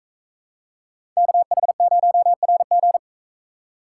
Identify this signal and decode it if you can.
Morse code, heard as a keyed tone, about 40 wpm, 700 Hz. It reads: KH0RG